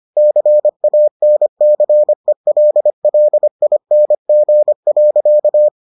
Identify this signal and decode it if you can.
Morse code, heard as a keyed tone, about 25 wpm, 600 Hz. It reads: CANCELLING.